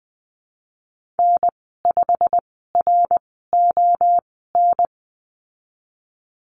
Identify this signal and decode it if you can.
Morse code, heard as a keyed tone, about 20 wpm, spaced as for 15 wpm, 700 Hz. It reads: N5RON
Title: 186 Lesson 30g - Letter b - Call Signs 20wpm